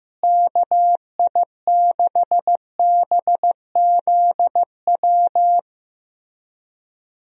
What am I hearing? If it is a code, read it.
Morse code, heard as a keyed tone, about 14 wpm, 700 Hz. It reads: KI6BZW
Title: USA Call Signs - 6 Characters - Repeated 3x - 15wpm